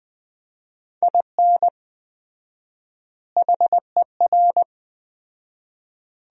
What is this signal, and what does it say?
Morse code, heard as a keyed tone, about 20 wpm, 700 Hz. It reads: IN HER
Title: 4x Word-Spacing - Sentences from Top 500 Words - 20wpm